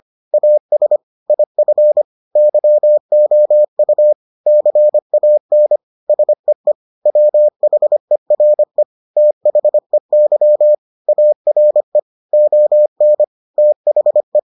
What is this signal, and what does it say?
Morse code, heard as a keyed tone, about 25 wpm, 600 Hz. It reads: AS IF YOU CAN SEE WHERE THEY ARE ON THE